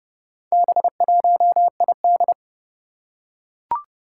Morse code, heard as a keyed tone, about 28 wpm, 690 Hz. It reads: B1ID
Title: Four Challenging Letter-Number Combinations 30wpm